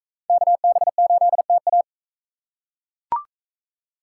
Morse code, heard as a keyed tone, about 40 wpm, 700 Hz. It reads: KB8TA